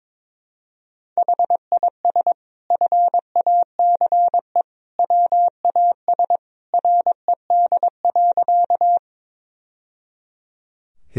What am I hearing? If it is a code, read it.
Morse code, heard as a keyed tone, about 22 wpm, 700 Hz. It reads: HIS FACE WAS RED.
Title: Sentences from Top 1000 Words - Review Part 2 - 22wpm